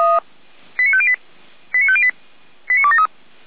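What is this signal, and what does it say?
Five-tone selective call (ZVEI), heard as quick runs of five tones, tones of about 70 ms. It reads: 89508 89508 89273
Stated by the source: ZVEI five-tone selective call received via SDR.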